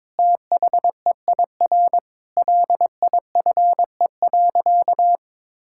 Morse code, heard as a keyed tone, about 22 wpm, 700 Hz. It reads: THEIR LIFE.
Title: Sentences from Top 1000 Words - Review Part 6 - 22wpm